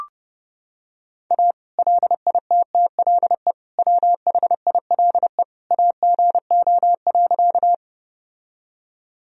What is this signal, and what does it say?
Morse code, heard as a keyed tone, about 30 wpm, 700 Hz. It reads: A LITTLE WHILE AGO.
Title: Sentences from Top 1000 Words - Review Part 3 - 30wpm